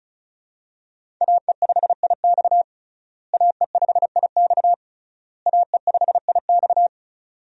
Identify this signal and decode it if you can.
Morse code, heard as a keyed tone, about 34 wpm, 700 Hz. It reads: AE5IX AE5IX AE5IX
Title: USA Call Signs - 5 Characters - Repeated 3x - 35wpm